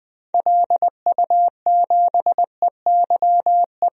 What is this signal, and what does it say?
Morse code, heard as a keyed tone, about 20 wpm, 700 Hz. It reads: LU7EYE
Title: DXCC-100 Call Signs - 6 Characters - 20wpm